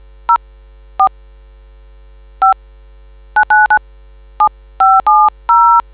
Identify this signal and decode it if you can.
DTMF, heard as telephone touch tones, tones of about 130 ms. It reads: 04 5997570